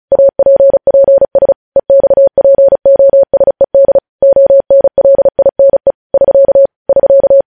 Morse code, heard as a keyed tone, about 36 wpm, 550 Hz. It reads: APPS EXPOSED ONLINE <SK> <SK>
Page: News headlines in Morse code at 35 wpm